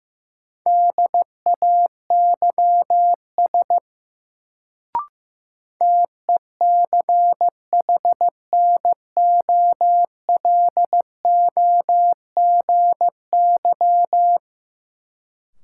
Morse code, heard as a keyed tone, about 15 wpm, 700 Hz. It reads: DAYS TECHNOLOGY
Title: Sentences from Top 1000 Words - Review Part 5 - New Word Drill - 15wpm